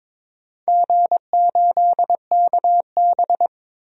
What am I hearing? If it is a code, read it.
Morse code, heard as a keyed tone, about 21 wpm, 700 Hz. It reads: G8KB
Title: DXCC-100 Call Signs - 4 Characters - 22wpm